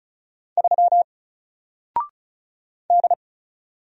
Morse code, heard as a keyed tone, about 34 wpm, 700 Hz. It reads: 3 D
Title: Single Letter-Number 35wpm